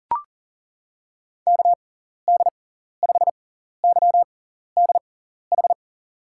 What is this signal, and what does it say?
Morse code, heard as a keyed tone, about 40 wpm, spaced as for 14 wpm, 700 Hz. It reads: KD5YDH
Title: USA Call Signs - 6 Characters - 6x Character Spacing - 40wpm